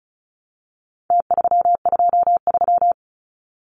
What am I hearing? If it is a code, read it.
Morse code, heard as a keyed tone, about 34 wpm, 700 Hz. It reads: T323
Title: Numbers - Contest Style - 4-Digit 35wpm